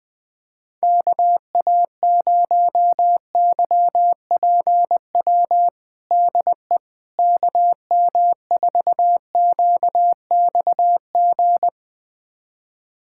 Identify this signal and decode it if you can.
Morse code, heard as a keyed tone, about 20 wpm, 700 Hz. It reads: KA0YPW DE KM4QXG
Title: USA Callsign DE Callsign - 6 Characters - 20wpm